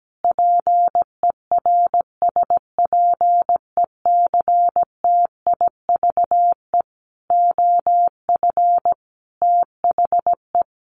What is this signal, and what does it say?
Morse code, heard as a keyed tone, about 17 wpm, 700 Hz. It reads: PERSPECTIVE OF THE
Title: Sets of 5 Words Encore2 17wpm